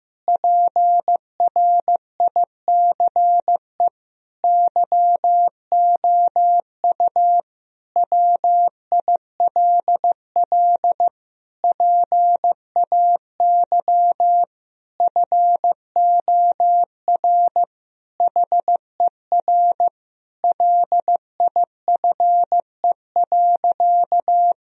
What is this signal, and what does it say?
Morse code, heard as a keyed tone, about 15 wpm, 700 Hz. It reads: PRICE YOU WILL PAY FOR HER LIFE.
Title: Sentences from Top 1000 Words - Review Part 4 - 15wpm